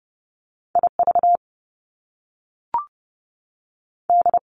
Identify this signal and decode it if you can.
Morse code, heard as a keyed tone, about 30 wpm, 700 Hz. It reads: I V D
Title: Top Two Letter Combinations in Words 30wpm